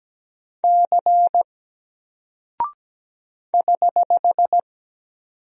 Morse code, heard as a keyed tone, about 17 wpm, 700 Hz. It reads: C <HH>
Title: Single Character (Letter:Number:Punctuation:Prosign) 17wpm